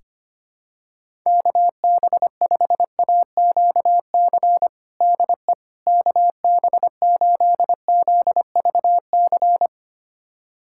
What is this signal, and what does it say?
Morse code, heard as a keyed tone, about 25 wpm, 700 Hz. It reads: KB5AQC DE KB8ZVC
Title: USA Callsign DE Callsign - 6 Characters - 25wpm